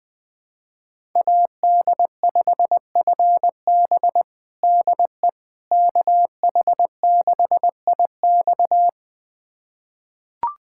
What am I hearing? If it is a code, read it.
Morse code, heard as a keyed tone, about 20 wpm, 700 Hz. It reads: AD5FB DE KH6IX
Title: USA Callsign DE Callsign - 5 Characters - 20wpm